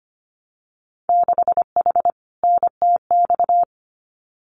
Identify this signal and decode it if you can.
Morse code, heard as a keyed tone, about 25 wpm, 700 Hz. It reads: 6H NTX
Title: Winter Field Day Exchanges Encore 25wpm